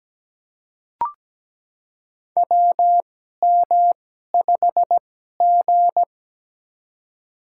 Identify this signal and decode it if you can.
Morse code, heard as a keyed tone, about 17 wpm, spaced as for 12 wpm, 700 Hz. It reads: WM5G
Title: USA Call Signs - 2x Character Spacing - 17wpm